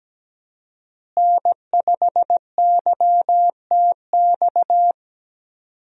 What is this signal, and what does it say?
Morse code, heard as a keyed tone, about 17 wpm, 700 Hz. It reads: N5YTX
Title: USA Call Signs - 5 Characters - Repeated 3x - 17wpm